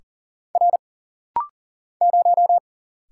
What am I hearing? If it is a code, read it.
Morse code, heard as a keyed tone, about 39 wpm, 700 Hz. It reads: R 0
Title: Single Letter-Number Alt - Rapid-Fire 40wpm